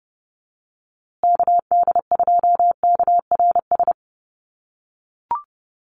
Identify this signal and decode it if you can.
Morse code, heard as a keyed tone, about 30 wpm, 700 Hz. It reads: KD2KRS
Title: USA Call Signs Repeated 2x 30wpm